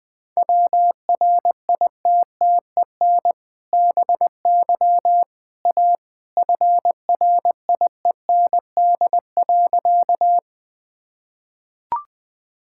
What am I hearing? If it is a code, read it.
Morse code, heard as a keyed tone, about 20 wpm, 700 Hz. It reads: WRITTEN BY A FRIEND.
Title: Sentences with 2-Word Phrases - 20wpm